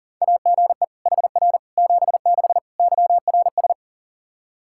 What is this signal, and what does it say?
Morse code, heard as a keyed tone, about 40 wpm, 700 Hz. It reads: AGE HR 76 YRS
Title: Practice QSO - Exchange 3 - More Info - 40wpm